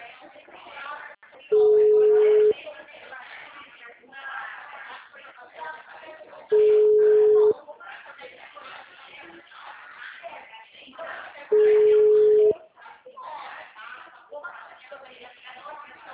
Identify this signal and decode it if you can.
Ringback tone (Europe, 425 Hz), with speech.
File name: out- -2010- - - .33799